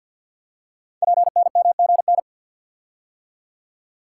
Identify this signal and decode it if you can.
Morse code, heard as a keyed tone, about 50 wpm, 700 Hz. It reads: PNMGN
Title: Code Groups - 5 Letters - 50wpm